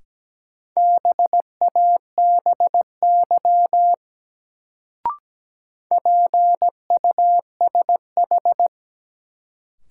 Morse code, heard as a keyed tone, about 17 wpm, 700 Hz. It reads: BABY PUSH
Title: Sentences from Top 1300 Words - New Word Drill Alt - 17wpm